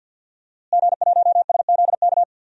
Morse code, heard as a keyed tone, about 47 wpm, 700 Hz. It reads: G1SZK
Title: DXCC-100 Call Signs - 5 Characters - Repeated 2x - 50wpm